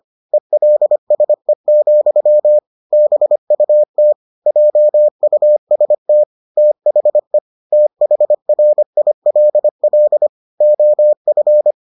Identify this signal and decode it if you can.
Morse code, heard as a keyed tone, about 25 wpm, 600 Hz. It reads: ELSE, BUT JUST THE THRILL OF